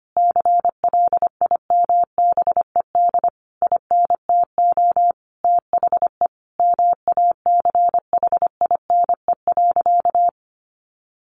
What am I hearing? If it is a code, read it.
Morse code, heard as a keyed tone, about 25 wpm, 700 Hz. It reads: CLIMBED INTO THE MACHINE.
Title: A Sound Of Thunder 25wpm